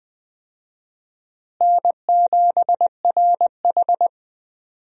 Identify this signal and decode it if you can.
Morse code, heard as a keyed tone, about 20 wpm, 700 Hz. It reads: N7RH
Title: USA Call Signs - 4 Characters - Repeated 3x - 20wpm